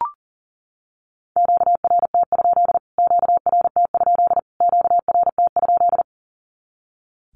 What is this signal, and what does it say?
Morse code, heard as a keyed tone, about 40 wpm, 700 Hz. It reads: QRT? QRT? QRT?
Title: Q-Codes Repeated 3x 40wpm